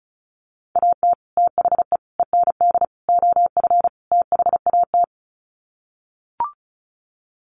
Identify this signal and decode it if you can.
Morse code, heard as a keyed tone, about 34 wpm, 700 Hz. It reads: AT THE END OF THAT
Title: Sets of 5 Words Encore2 35wpm